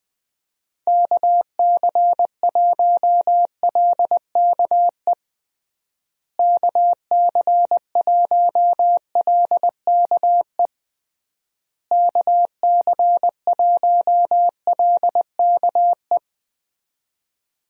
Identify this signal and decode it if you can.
Morse code, heard as a keyed tone, about 20 wpm, 700 Hz. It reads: KC1LKE KC1LKE KC1LKE
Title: USA Call Signs - 6 Characters - Repeated 3x - 20wpm